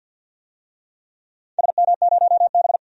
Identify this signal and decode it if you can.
Morse code, heard as a keyed tone, about 47 wpm, 700 Hz. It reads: SM0B
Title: DXCC-100 Call Signs - 4 Characters - 50wpm